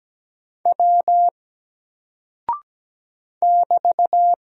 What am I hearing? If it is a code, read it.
Morse code, heard as a keyed tone, about 17 wpm, 700 Hz. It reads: W <BT>
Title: Single Character (Letter:Number:Punctuation:Prosign) 17wpm